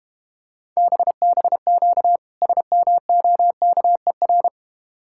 Morse code, heard as a keyed tone, about 32 wpm, 700 Hz. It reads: BBQ SMOKER